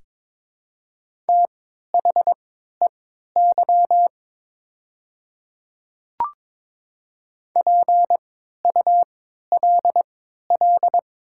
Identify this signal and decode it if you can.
Morse code, heard as a keyed tone, about 22 wpm, spaced as for 12 wpm, 700 Hz. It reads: THEY PULL
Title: 3x Character Spacing - Top 500 Words - 22wpm